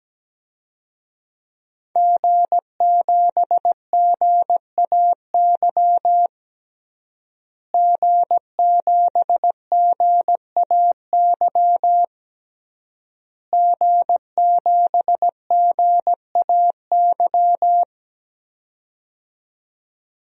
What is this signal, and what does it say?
Morse code, heard as a keyed tone, about 17 wpm, 700 Hz. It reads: G7GAY G7GAY G7GAY
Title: DXCC-100 Call Signs - 5 Characters - Repeated 3x - 17wpm